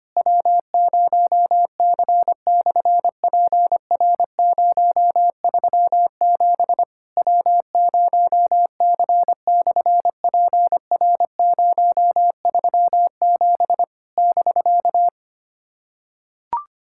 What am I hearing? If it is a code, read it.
Morse code, heard as a keyed tone, about 24 wpm, 700 Hz. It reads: W0C/PR037 W0C/PR037 <BK>
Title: SOTA QSO Practice 25wpm